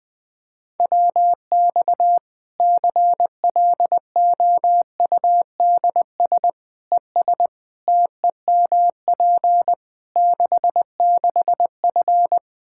Morse code, heard as a keyed tone, about 21 wpm, 700 Hz. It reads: WX CLOUDS ES TEMP 66F
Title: Practice QSO - Exchange 2 - Info - 20wpm